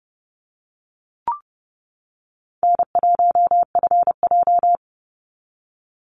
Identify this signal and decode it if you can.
Morse code, heard as a keyed tone, about 30 wpm, 700 Hz. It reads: N1FJ
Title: NA SOTA Activator Callsigns 30wpm